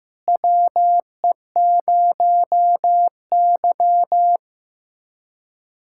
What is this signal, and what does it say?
Morse code, heard as a keyed tone, about 15 wpm, 700 Hz. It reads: WE0Y
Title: USA Callsign DE Callsign - 4 Characters - 15wpm